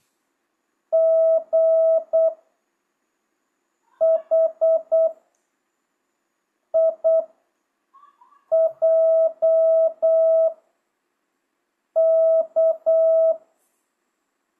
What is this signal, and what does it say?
Morse code, heard as a keyed tone, about 8 wpm, spaced as for 4 wpm, 660 Hz. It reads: GHIJK
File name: jh8cbh / 2025cw